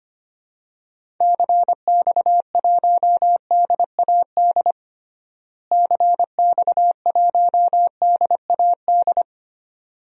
Morse code, heard as a keyed tone, about 25 wpm, 700 Hz. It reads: CX1DAD CX1DAD
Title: DXCC-100 Call Signs - 6 Characters - Repeated 2x - 25wpm